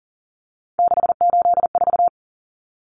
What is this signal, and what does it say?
Morse code, heard as a keyed tone, about 40 wpm, 700 Hz. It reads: 684
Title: Numbers - 3-Digit 40wpm